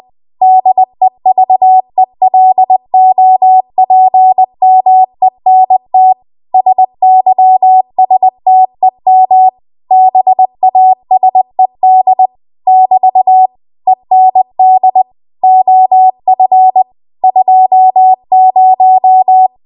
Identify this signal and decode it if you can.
Morse code, heard as a keyed tone, about 20 wpm, 750 Hz. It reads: DEVELOPMENT SYSTEM BASED <BT> END OF 20